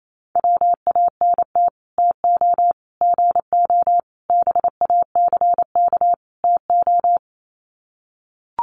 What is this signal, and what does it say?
Morse code, heard as a keyed tone, about 28 wpm, 700 Hz. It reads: WANT TO GO BACK TO
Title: Sets of 5 Words 28wpm